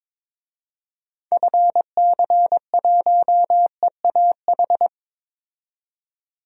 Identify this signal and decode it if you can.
Morse code, heard as a keyed tone, about 22 wpm, 700 Hz. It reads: FC1EAH
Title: DXCC-100 Call Signs - 6 Characters - 22wpm